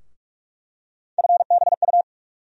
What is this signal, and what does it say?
Morse code, heard as a keyed tone, about 46 wpm, 700 Hz. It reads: FBU
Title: USA Call Sign Postfixes - 3 Letters 45wpm